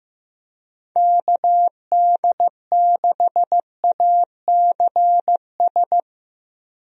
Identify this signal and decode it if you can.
Morse code, heard as a keyed tone, about 15 wpm, 700 Hz. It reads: KD6ACS
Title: USA Call Signs - 6 Characters - 15wpm